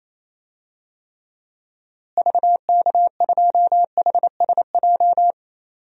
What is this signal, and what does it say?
Morse code, heard as a keyed tone, about 28 wpm, 700 Hz. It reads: VK2HSJ
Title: DXCC-100 Call Signs - 6 Characters - Repeated 3x - 28wpm